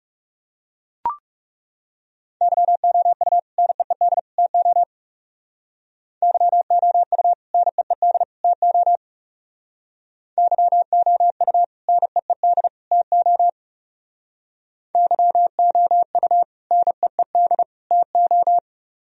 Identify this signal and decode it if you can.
Morse code, heard as a keyed tone, about 36 wpm, 700 Hz. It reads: YOU NEED TO YOU NEED TO YOU NEED TO YOU NEED TO
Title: Speed-Racing - Sets of 3 Words 30wpm